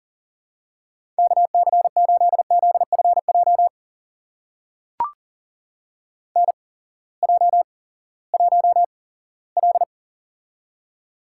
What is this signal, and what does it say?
Morse code, heard as a keyed tone, about 40 wpm, spaced as for 20 wpm, 700 Hz. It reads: KC8ZFJ NJ1L
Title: USA Call Signs - 8x Character Spacing - 40wpm